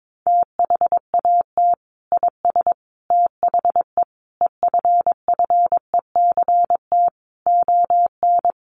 Morse code, heard as a keyed tone, about 22 wpm, 700 Hz. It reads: THAT IS THE EFFECT ON
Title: Sentences from Top 1300 Words 22wpm